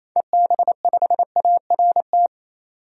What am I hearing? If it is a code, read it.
Morse code, heard as a keyed tone, about 28 wpm, 700 Hz. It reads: EB5ART
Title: DXCC-100 Call Signs - 6 Characters - 28wpm